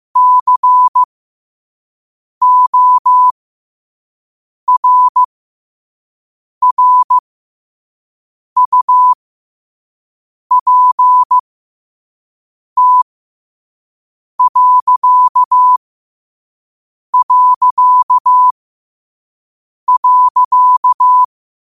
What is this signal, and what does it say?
Morse code, heard as a keyed tone, about 15 wpm, spaced as for 5 wpm, 1000 Hz. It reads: CORRUPT...
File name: qotdfiles / 5wpm